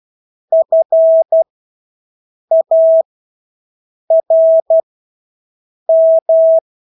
Morse code, heard as a keyed tone, about 12 wpm, spaced as for 6 wpm, 650 Hz. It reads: FARM
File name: Engelske ord 12-6 wpm